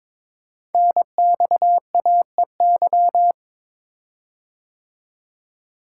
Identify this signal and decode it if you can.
Morse code, heard as a keyed tone, about 22 wpm, 700 Hz. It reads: NXAEY
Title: Code Groups - 5 Letters - 22wpm